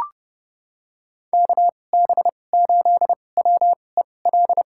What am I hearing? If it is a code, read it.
Morse code, heard as a keyed tone, about 32 wpm, spaced as for 22 wpm, 700 Hz. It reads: KB8WEL
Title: USA Call Signs - 2x Character Spacing - 30wpm